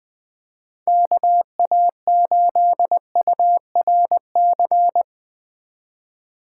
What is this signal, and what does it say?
Morse code, heard as a keyed tone, about 20 wpm, 700 Hz. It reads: KA8URC
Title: USA Call Signs - 6 Characters - 3x Character Spacing - 20wpm